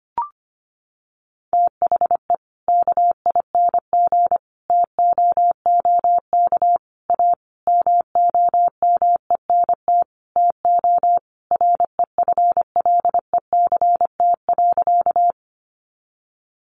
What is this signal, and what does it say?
Morse code, heard as a keyed tone, about 25 wpm, 700 Hz. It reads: THE KING TOOK A MOMENT TO REFLECT.
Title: Sentences from Top 1500 Words 25wpm